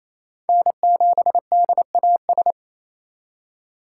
Morse code, heard as a keyed tone, about 28 wpm, 700 Hz. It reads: N7DAS
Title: USA Call Signs - 5 Characters - 4x Character Spacing - 28wpm